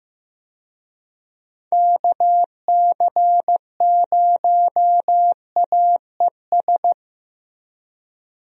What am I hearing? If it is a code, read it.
Morse code, heard as a keyed tone, about 15 wpm, 700 Hz. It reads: KC0AES
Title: USA Call Signs - 6 Characters - Repeated 3x - 15wpm